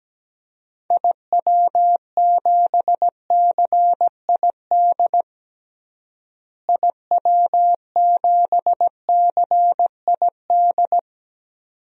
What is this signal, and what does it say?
Morse code, heard as a keyed tone, about 17 wpm, 700 Hz. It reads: IW7CID IW7CID
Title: DXCC-100 Call Signs - 6 Characters - Repeated 2x - 17wpm